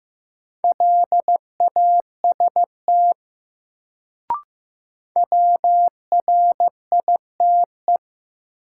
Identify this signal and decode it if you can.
Morse code, heard as a keyed tone, about 15 wpm, 700 Hz. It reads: LAST WRITE
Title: Sentences from Top 300 Words - New Word Drill - 15wpm